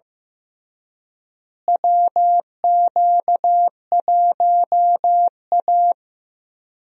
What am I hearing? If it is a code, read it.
Morse code, heard as a keyed tone, about 15 wpm, 700 Hz. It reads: WQ1A
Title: USA Call Signs - 4 Characters - Repeated 2x - 15wpm